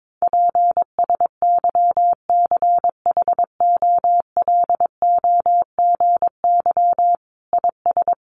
Morse code, heard as a keyed tone, about 22 wpm, 700 Hz. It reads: PSYCHOLOGY IS
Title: Sentences from Top 1400 Words 22wpm